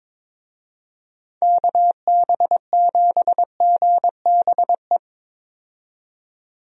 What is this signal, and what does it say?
Morse code, heard as a keyed tone, about 22 wpm, 700 Hz. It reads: KB7GBE
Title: USA Call Signs - 6 Characters - 22wpm